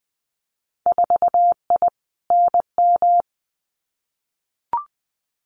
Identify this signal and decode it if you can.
Morse code, heard as a keyed tone, about 19 wpm, 700 Hz. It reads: 4I NM
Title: Winter Field Day Exchanges Encore 20wpm